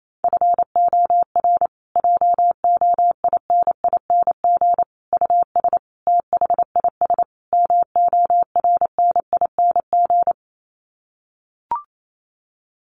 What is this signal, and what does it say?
Morse code, heard as a keyed tone, about 28 wpm, 700 Hz. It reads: FOR JOINING US THIS MORNING
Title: Sets of 5 Words Encore 28wpm